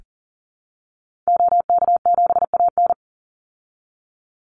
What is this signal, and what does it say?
Morse code, heard as a keyed tone, about 40 wpm, 700 Hz. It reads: OK7AN
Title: Call Signs Repeated 2x - 40wpm